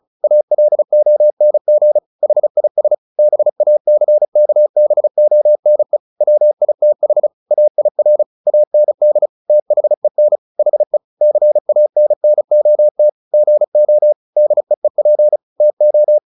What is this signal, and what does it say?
Morse code, heard as a keyed tone, about 34 wpm, 600 Hz. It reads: ALONG HIS BACKBONE WITH AIR AND THEN HE CANNOT GO DEEP TO
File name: the old man and the sea / the old man and the sea 1 35wpm